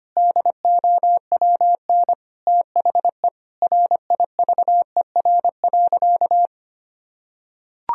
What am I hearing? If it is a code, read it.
Morse code, heard as a keyed tone, about 25 wpm, 700 Hz. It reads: DOWN THE RIVER.
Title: Sentences from Top 1000 Words - Review Part 1 - 25wpm